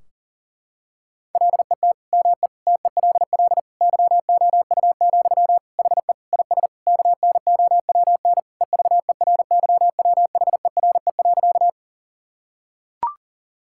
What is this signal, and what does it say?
Morse code, heard as a keyed tone, about 40 wpm, 700 Hz. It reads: LET ME TELL YOU, HE IS KNOWN EVERYWHERE.
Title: Sentences from Top 1600 Words 40wpm